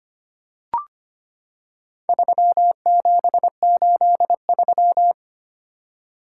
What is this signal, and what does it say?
Morse code, heard as a keyed tone, about 25 wpm, 700 Hz. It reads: 3783
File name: practice / 25wpm / Four Letter-Number Combinations - 25wpm